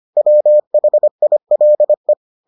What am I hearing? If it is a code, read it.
Morse code, heard as a keyed tone, about 25 wpm, 600 Hz. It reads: WHILE